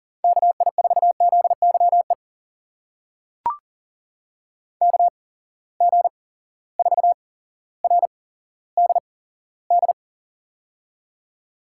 Morse code, heard as a keyed tone, about 40 wpm, spaced as for 17 wpm, 700 Hz. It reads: KI4ZYE KG4RDD
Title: USA Call Signs - 6 Characters - 8x Character Spacing - 40wpm